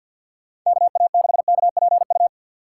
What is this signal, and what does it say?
Morse code, heard as a keyed tone, about 52 wpm, 700 Hz. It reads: KA6KPU
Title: USA Call Signs - 3x Character Spacing - 50wpm